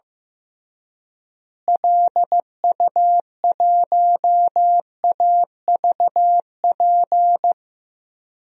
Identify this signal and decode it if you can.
Morse code, heard as a keyed tone, about 15 wpm, 700 Hz. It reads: LU1AVP
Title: DXCC-100 Call Signs - 6 Characters - Repeated 3x - 15wpm